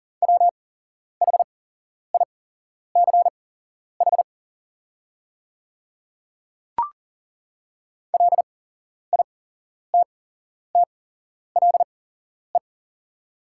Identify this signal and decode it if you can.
Morse code, heard as a keyed tone, about 40 wpm, spaced as for 11 wpm, 690 Hz. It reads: WHICH LITTLE
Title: 8x Character Spacing - Top 200 Words - 40wpm